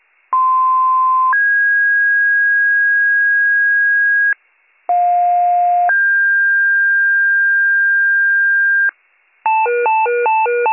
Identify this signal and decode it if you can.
Two-tone page, heard as a one-second tone, then a three-second tone, about 1040 then 1650 Hz, about 710 then 1600 Hz. It is heard from 0.3 to 8.9 s.